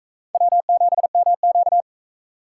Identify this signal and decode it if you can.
Morse code, heard as a keyed tone, about 44 wpm, 700 Hz. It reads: W7MQ